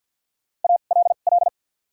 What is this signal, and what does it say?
Morse code, heard as a keyed tone, about 53 wpm, spaced as for 37 wpm, 700 Hz. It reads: ALL
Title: 2x Character Spacing - Top 100 Words - 50wpm